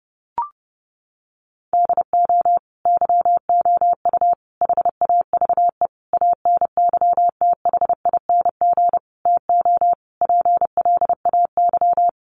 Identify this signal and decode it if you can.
Morse code, heard as a keyed tone, about 30 wpm, 700 Hz. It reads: DO YOU HAVE ANYTHING TO PLAY
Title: Sentences from Top 1000 Words 30wpm